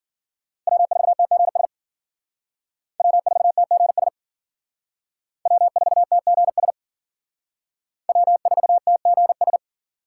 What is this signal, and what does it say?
Morse code, heard as a keyed tone, about 50 wpm, 700 Hz. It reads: W4TGS W4TGS W4TGS W4TGS